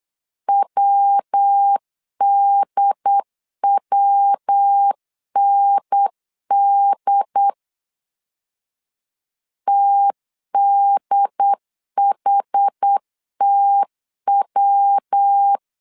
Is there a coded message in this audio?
Yes: Morse code from 0.5 to 15.6 s, heard as a keyed tone, about 8 wpm, 800 Hz. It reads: WDWND TDHTW